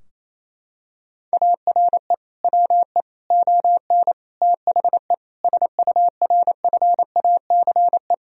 Morse code, heard as a keyed tone, about 28 wpm, 700 Hz. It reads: ARE WE ON THE SURFACE
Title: Sentences from Top 1000 Words - Review Part 3 - 28wpm